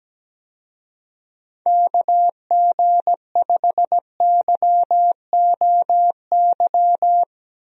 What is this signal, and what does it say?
Morse code, heard as a keyed tone, about 17 wpm, 700 Hz. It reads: KG5YOY
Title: USA Call Signs - 6 Characters - Repeated 3x - 17wpm